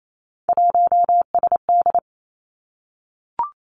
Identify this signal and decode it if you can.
Morse code, heard as a keyed tone, about 28 wpm, 700 Hz. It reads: 1SD
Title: Three Challenging Letter-Number Combinations 28wpm